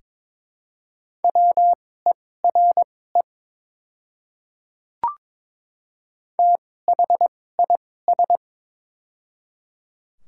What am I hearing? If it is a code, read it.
Morse code, heard as a keyed tone, about 22 wpm, spaced as for 16 wpm, 700 Hz. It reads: WERE THIS
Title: 2x Character Spacing - Top 100 Words - 22wpm